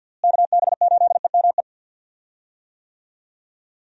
Morse code, heard as a keyed tone, about 52 wpm, 700 Hz. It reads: KB8EME
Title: USA Call Signs - 6 Characters - Repeated 3x - 50wpm